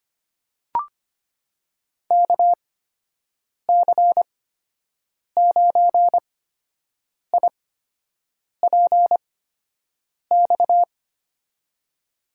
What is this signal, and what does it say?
Morse code, heard as a keyed tone, about 25 wpm, spaced as for 7 wpm, 700 Hz. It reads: KC9IPX
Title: USA Call Signs - 6 Characters - 8x Character Spacing - 25wpm